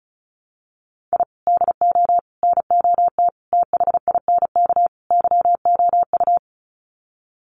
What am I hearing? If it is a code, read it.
Morse code, heard as a keyed tone, about 34 wpm, 700 Hz. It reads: I DO NOT THINK YOU
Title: Sets of 5 Words 35wpm